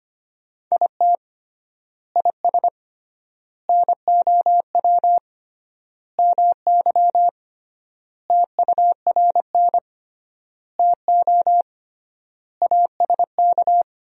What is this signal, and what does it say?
Morse code, heard as a keyed tone, about 25 wpm, 700 Hz. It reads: IT IS NOW MY TURN TO ASK
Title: 3x Word-Spacing - Sentences from Top 200 Words - 25wpm